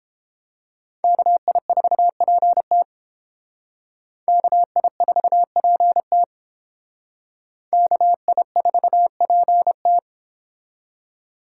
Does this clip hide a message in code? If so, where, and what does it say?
Yes: Morse code from 1.0 to 10.0 s, heard as a keyed tone, about 31 wpm, 700 Hz. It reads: KI4PT KI4PT KI4PT